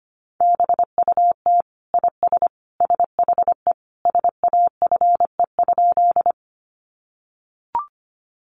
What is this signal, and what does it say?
Morse code, heard as a keyed tone, about 25 wpm, 700 Hz. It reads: BUT IS SHE SAFE?
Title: Sentences from Top 900 Words 25wpm